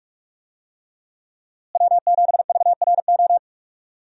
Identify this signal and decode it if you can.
Morse code, heard as a keyed tone, about 46 wpm, 700 Hz. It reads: W7VRO
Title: USA Call Signs - 5 Characters - Repeated 3x - 45wpm